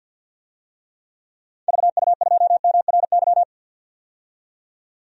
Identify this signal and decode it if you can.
Morse code, heard as a keyed tone, about 47 wpm, 700 Hz. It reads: VU2MRY
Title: DXCC-100 Call Signs - 6 Characters - 50wpm